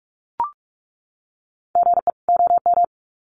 Morse code, heard as a keyed tone, about 44 wpm, 700 Hz. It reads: GE OM
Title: Common QSO Phrases 45wpm